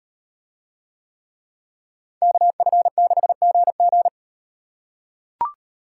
Morse code, heard as a keyed tone, about 35 wpm, 700 Hz. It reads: KF6GG